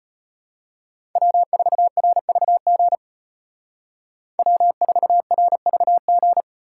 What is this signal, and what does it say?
Morse code, heard as a keyed tone, about 33 wpm, 700 Hz. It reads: W4RVG W4RVG